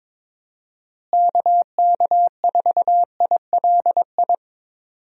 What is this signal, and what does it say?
Morse code, heard as a keyed tone, about 22 wpm, 700 Hz. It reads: KK4ILI
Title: USA Call Signs - 6 Characters - Repeated 3x - 22wpm